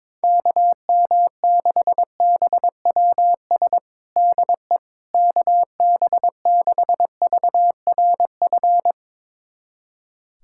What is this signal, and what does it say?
Morse code, heard as a keyed tone, about 22 wpm, 700 Hz. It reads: KM6BWS DE KB6VRF
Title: USA Callsign DE Callsign - 6 Characters - 22wpm